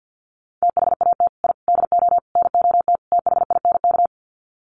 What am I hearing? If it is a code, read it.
Morse code, heard as a keyed tone, about 50 wpm, 700 Hz. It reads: THAT I DO NOT THINK
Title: Sets of 5 Words Encore 50wpm